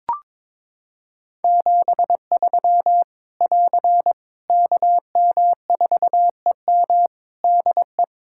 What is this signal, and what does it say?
Morse code, heard as a keyed tone, about 23 wpm, 700 Hz. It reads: 73 <AR> KM4EM DE
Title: Practice QSO - Exchange 4 - Ending - 22wpm